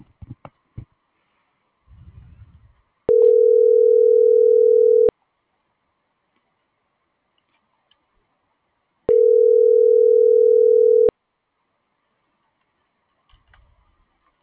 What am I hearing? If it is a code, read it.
Ringback tone (North America), 440 + 480 Hz.